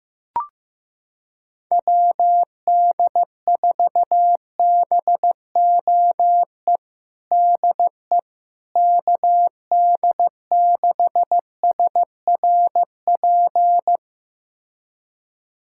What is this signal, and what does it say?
Morse code, heard as a keyed tone, about 15 wpm, 700 Hz. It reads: WD4BOE DE KD6SRP
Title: USA Callsign DE Callsign 15wpm